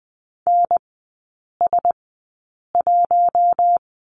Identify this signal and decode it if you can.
Morse code, heard as a keyed tone, about 20 wpm, spaced as for 8 wpm, 700 Hz. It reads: NS1
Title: 067 Lesson 12e - Letter d - Three Characters 20wpm